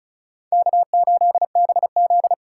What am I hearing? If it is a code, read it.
Morse code, heard as a keyed tone, about 34 wpm, 700 Hz. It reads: K8BZ
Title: K1USN SST QSO Practice 35wpm